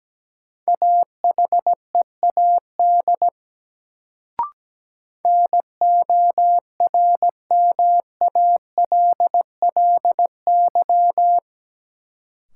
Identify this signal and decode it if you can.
Morse code, heard as a keyed tone, about 17 wpm, 700 Hz. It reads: AHEAD NORMALLY
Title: Sentences from Top 1000 Words - Review Part 6 - New Word Drill - 17wpm